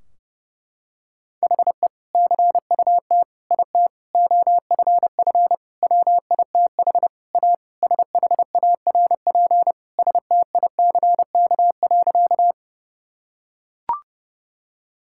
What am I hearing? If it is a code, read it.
Morse code, heard as a keyed tone, about 32 wpm, 700 Hz. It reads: HE CUT IT OFF WITH A SHARP STICK.
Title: Sentences from Top 1600 Words 30wpm